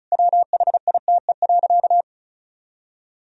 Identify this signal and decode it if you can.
Morse code, heard as a keyed tone, about 36 wpm, 700 Hz. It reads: WHITE.
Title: Sentences from Top 1000 Words - Review Part 2 - 35wpm